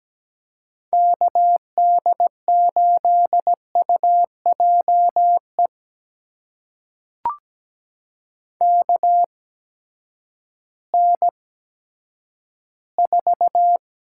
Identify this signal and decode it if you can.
Morse code, heard as a keyed tone, about 17 wpm, spaced as for 10 wpm, 700 Hz. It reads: KD8UJE KN4
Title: USA Call Signs - 6 Characters - 8x Character Spacing - 17wpm